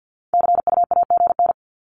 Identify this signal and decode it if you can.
Morse code, heard as a keyed tone, about 49 wpm, 700 Hz. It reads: CUAGN
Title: Abbreviations 50wpm